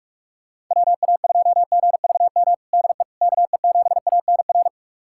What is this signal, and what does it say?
Morse code, heard as a keyed tone, about 47 wpm, 700 Hz. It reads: WA2GVM DE KE7ANR
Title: USA Callsign DE Callsign - 6 Characters - 45wpm